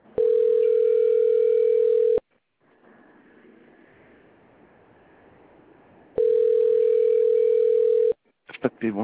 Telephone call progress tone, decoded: ringback tone